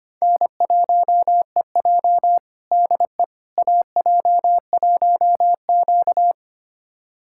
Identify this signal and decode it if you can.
Morse code, heard as a keyed tone, about 25 wpm, 700 Hz. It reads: N1EJ DE AJ1Q
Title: USA Callsign DE Callsign - 4 Characters - 25wpm